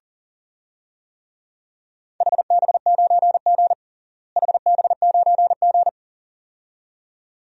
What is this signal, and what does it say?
Morse code, heard as a keyed tone, about 37 wpm, 700 Hz. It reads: HB9G HB9G
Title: DXCC-100 Call Signs - 4 Characters - Repeated 2x - 40wpm